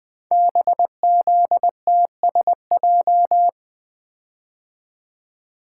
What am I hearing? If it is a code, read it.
Morse code, heard as a keyed tone, about 20 wpm, 700 Hz. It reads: BZTSJ
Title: Code Groups - 5 Letters - 20wpm